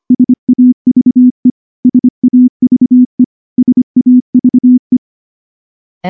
Morse code, heard as a keyed tone, about 24 wpm, 250 Hz. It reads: SAVE SAVE SAVE
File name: morse / morse-training-25WPM